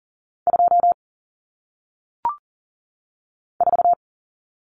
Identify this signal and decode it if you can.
Morse code, heard as a keyed tone, about 40 wpm, 700 Hz. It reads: 2 4
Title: Common Cut Numbers - 1-Digit 40wpm